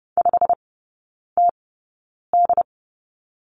Morse code, heard as a keyed tone, about 30 wpm, spaced as for 9 wpm, 700 Hz. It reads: 5TD
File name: course / 30:10wpm / 087 Lesson 15e - Number 5 - Three Characters 30wpm